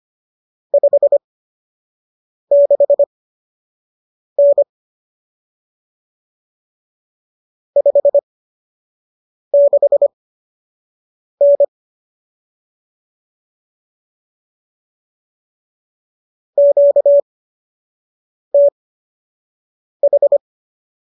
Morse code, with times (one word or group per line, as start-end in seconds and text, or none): 0.7-4.6 56N
7.8-11.6 56N
16.6-20.4 QTH